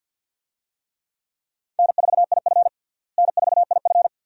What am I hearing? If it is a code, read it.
Morse code, heard as a keyed tone, about 51 wpm, 700 Hz. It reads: N4IF N4IF
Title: USA Call Signs - 4 Characters - Repeated 2x - 50wpm